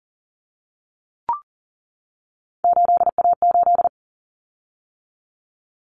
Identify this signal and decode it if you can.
Morse code, heard as a keyed tone, about 40 wpm, 700 Hz. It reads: 8A8
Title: Common Cut Numbers - 3-Digit 40wpm